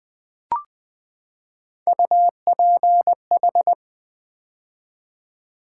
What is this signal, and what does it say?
Morse code, heard as a keyed tone, about 20 wpm, 700 Hz. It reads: UPH
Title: Build A Word - 3 Letters At A Time - 6-Letter Words - 20wpm